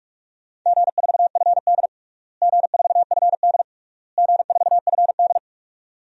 Morse code, heard as a keyed tone, about 46 wpm, 700 Hz. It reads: G4FD G4FD G4FD
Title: DXCC-100 Call Signs - 4 Characters - Repeated 3x - 45wpm